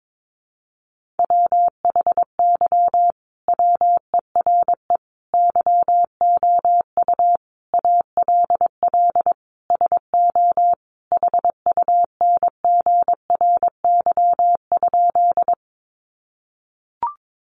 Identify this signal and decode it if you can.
Morse code, heard as a keyed tone, about 22 wpm, 700 Hz. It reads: WHY WERE YOU ALL SO HUNGRY?
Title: Sentences from Top 1400 Words 22wpm